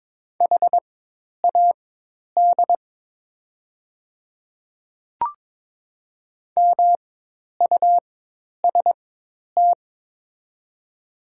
Morse code, heard as a keyed tone, about 21 wpm, spaced as for 10 wpm, 700 Hz. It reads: HAD MUST
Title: 4x Character Spacing - Top 500 Words - 22wpm